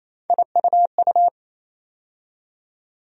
Morse code, heard as a keyed tone, about 28 wpm, 700 Hz. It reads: IUU
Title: USA Call Sign Postfixes - 3 Letters 28wpm